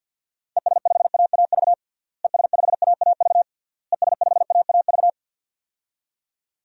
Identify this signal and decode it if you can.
Morse code, heard as a keyed tone, about 52 wpm, 700 Hz. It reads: ES5AAV ES5AAV ES5AAV
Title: DXCC-100 Call Signs - 6 Characters - Repeated 3x - 50wpm